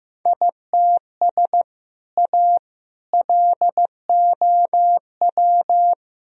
Morse code, heard as a keyed tone, about 15 wpm, 700 Hz. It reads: ITS A LOW
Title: Sentences from Top 1000 Words - Review Part 1 - 15wpm